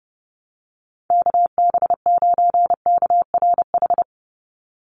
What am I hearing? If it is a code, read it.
Morse code, heard as a keyed tone, about 30 wpm, 700 Hz. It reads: KB9KRH
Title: USA Call Signs 30wpm